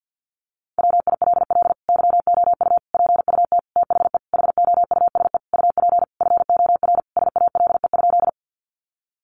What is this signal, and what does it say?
Morse code, heard as a keyed tone, about 50 wpm, 700 Hz. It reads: WILL YOU PUT THE HOUSE UP FOR SALE?
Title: Sentences from Top 1400 Words 50wpm